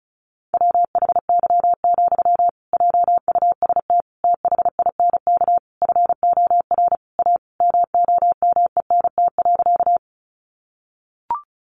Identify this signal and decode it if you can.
Morse code, heard as a keyed tone, about 34 wpm, 700 Hz. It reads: WHY, JUST THINK FOR A MOMENT.
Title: Sentences from Top 1000 Words 35wpm